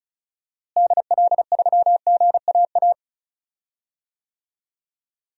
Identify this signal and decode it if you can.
Morse code, heard as a keyed tone, about 34 wpm, 700 Hz. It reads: DL3GAA
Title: DXCC-100 Call Signs - 6 Characters - Repeated 3x - 35wpm